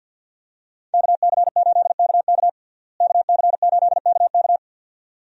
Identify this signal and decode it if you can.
Morse code, heard as a keyed tone, about 49 wpm, 700 Hz. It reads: KC8KK KC8KK
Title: USA Call Signs - 5 Characters - Repeated 2x - 50wpm